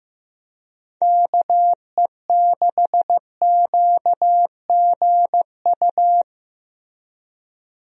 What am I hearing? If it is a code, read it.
Morse code, heard as a keyed tone, about 15 wpm, 700 Hz. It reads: KE6QGU
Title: USA Call Signs - 6 Characters - Repeated 3x - 15wpm